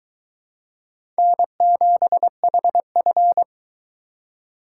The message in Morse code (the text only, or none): N7HF